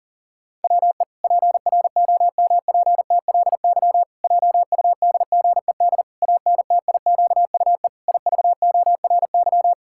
Morse code, heard as a keyed tone, about 40 wpm, 700 Hz. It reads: WE PROMPTLY JUDGED ANTIQUE IVORY
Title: Pangram Challenge v2 - 40wpm